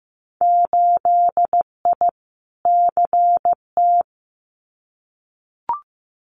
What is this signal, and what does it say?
Morse code, heard as a keyed tone, about 15 wpm, 700 Hz. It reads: 8I CT
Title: Winter Field Day Exchanges Encore 15wpm